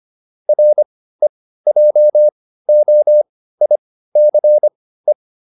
Morse code, heard as a keyed tone, about 25 wpm, spaced as for 15 wpm, 600 Hz. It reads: REJOICE